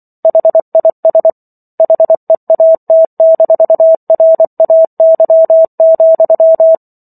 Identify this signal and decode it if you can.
Morse code, heard as a keyed tone, about 24 wpm, 650 Hz. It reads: HIS HEAT-RAY,